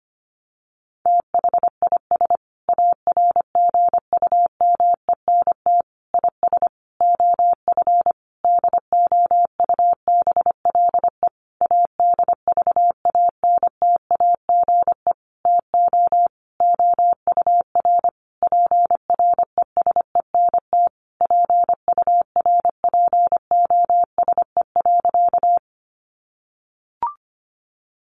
Morse code, heard as a keyed tone, about 25 wpm, 700 Hz. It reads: THIS ARGUMENT IS OF DOUBLE ADVANTAGE TO OUR PRESENT PURPOSE.
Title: Sentences from Top 1400 Words 25wpm